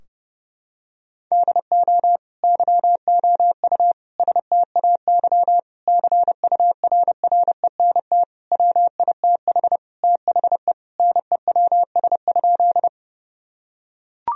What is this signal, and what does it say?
Morse code, heard as a keyed tone, about 30 wpm, 700 Hz. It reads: DO YOU STAY CURRENT WITH THE NEWS?
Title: Sentences from Top 1000 Words - Review Part 4 - 30wpm